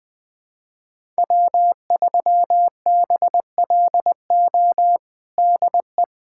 Morse code, heard as a keyed tone, about 20 wpm, 700 Hz. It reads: W3BLO DE
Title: USA Callsign DE Callsign - 5 Characters - 20wpm